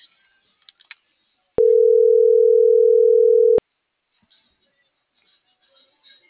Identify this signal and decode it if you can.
Ringback tone (North America), 440 + 480 Hz.